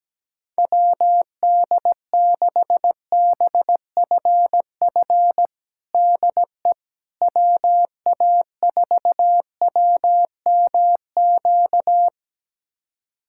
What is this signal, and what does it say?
Morse code, heard as a keyed tone, about 17 wpm, 700 Hz. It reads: WD6BFF DE WA4WMQ
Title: USA Callsign DE Callsign - 6 Characters - 17wpm